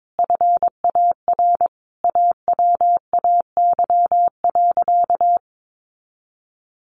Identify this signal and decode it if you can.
Morse code, heard as a keyed tone, about 22 wpm, 700 Hz. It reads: FAR AWAY.
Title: Additional Easy Phrases 22wpm